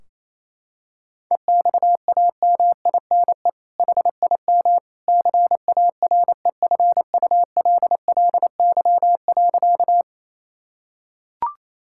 Morse code, heard as a keyed tone, about 28 wpm, 700 Hz. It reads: EXAMINE HIM CAREFULLY.
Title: Sentences from Top 1600 Words 28wpm